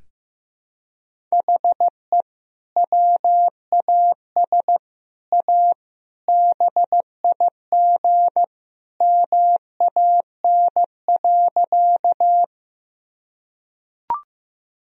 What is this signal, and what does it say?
Morse code, heard as a keyed tone, about 15 wpm, 700 Hz. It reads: HE WAS A BIG MAN.
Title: Sentences from Top 1000 Words - Review Part 1 - 15wpm